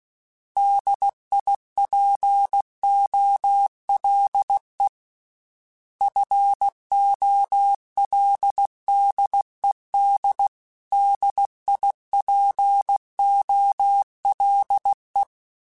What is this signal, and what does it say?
Morse code, heard as a keyed tone, about 16 wpm, 800 Hz. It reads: DIPOLE FOLDED DIPOLE
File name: fromSMs / FromSMs mp3s / Antennas@15